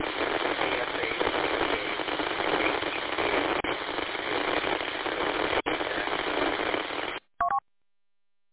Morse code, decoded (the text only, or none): none